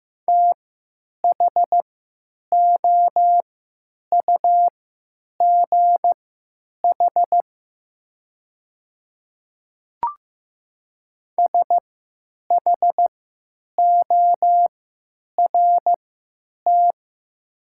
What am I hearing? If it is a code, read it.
Morse code, heard as a keyed tone, about 15 wpm, spaced as for 9 wpm, 700 Hz. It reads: THOUGH SHORT
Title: 3x Character Spacing - Top 500 Words - 15wpm